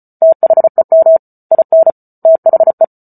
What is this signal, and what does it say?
Morse code, heard as a keyed tone, about 34 wpm, 650 Hz. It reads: THEM IN THE
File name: Book Two / 08-Chapter 8 - 34 WPM